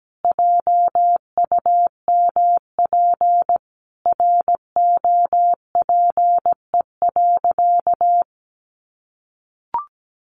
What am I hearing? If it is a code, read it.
Morse code, heard as a keyed tone, about 17 wpm, 700 Hz. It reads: JUMP ROPE.
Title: Easy Phrases 17wpm